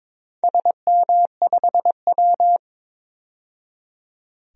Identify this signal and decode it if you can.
Morse code, heard as a keyed tone, about 22 wpm, 700 Hz. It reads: SM5W
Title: DXCC-100 Call Signs - 4 Characters - 22wpm